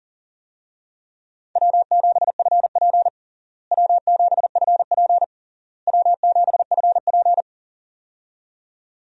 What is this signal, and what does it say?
Morse code, heard as a keyed tone, about 40 wpm, 700 Hz. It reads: W7FP W7FP W7FP
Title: USA Call Signs - 4 Characters - Repeated 3x - 40wpm